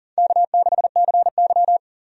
Morse code, heard as a keyed tone, about 40 wpm, 700 Hz. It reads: K6CY
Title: USA Call Signs - 4 Characters - 40wpm